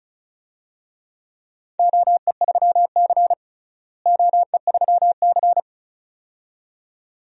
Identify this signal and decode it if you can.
Morse code, heard as a keyed tone, about 34 wpm, 700 Hz. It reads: OE3C OE3C
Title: DXCC-100 Call Signs - 4 Characters - Repeated 2x - 35wpm